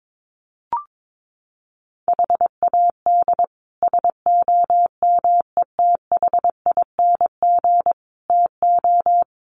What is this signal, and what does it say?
Morse code, heard as a keyed tone, about 22 wpm, 700 Hz. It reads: HAD SOMETHING TO
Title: Sets of 5 Words 22wpm